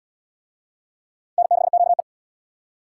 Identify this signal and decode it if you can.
Morse code, heard as a keyed tone, about 70 wpm, 700 Hz. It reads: N6QE